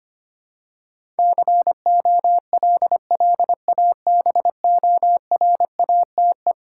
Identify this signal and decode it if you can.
Morse code, heard as a keyed tone, about 25 wpm, 700 Hz. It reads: COLLABORATE
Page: morse-code-ninja-practice-25wpm